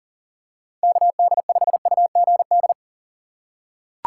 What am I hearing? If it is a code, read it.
Morse code, heard as a keyed tone, about 40 wpm, 700 Hz. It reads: KD5UGD
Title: USA Call Signs - 6 Characters - 2x Character Spacing - 40wpm